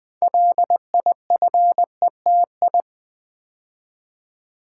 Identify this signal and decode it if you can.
Morse code, heard as a keyed tone, about 20 wpm, 700 Hz. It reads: LIFETI
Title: Build A Word - 2 Letters At A Time - 8-Letter Words - 20wpm